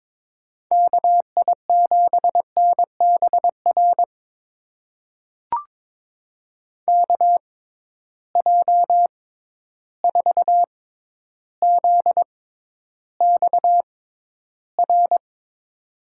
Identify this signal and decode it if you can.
Morse code, heard as a keyed tone, about 22 wpm, spaced as for 11 wpm, 700 Hz. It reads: KI7NBR KJ4ZXR
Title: USA Call Signs - 6x Character Spacing - 22wpm